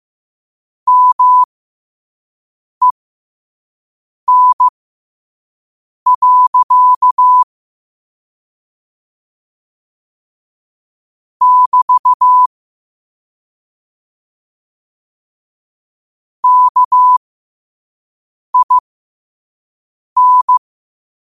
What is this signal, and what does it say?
Morse code, heard as a keyed tone, about 15 wpm, spaced as for 5 wpm, 1000 Hz. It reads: MEN. <BT> KIN